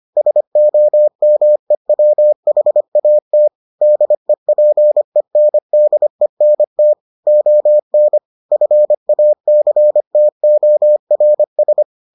Morse code, heard as a keyed tone, about 25 wpm, 600 Hz. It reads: SOMEWHAT DEPENDENT ON FACTORS